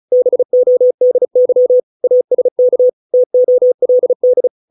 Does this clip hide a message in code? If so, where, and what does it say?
Yes: Morse code from 0.1 to 4.5 s, heard as a keyed tone, about 34 wpm, 500 Hz. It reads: BODY ASK TOLD